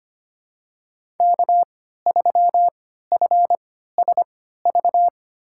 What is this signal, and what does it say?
Morse code, heard as a keyed tone, about 25 wpm, spaced as for 14 wpm, 700 Hz. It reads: K3FSV
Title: USA Call Signs - 5 Characters - 3x Character Spacing - 25wpm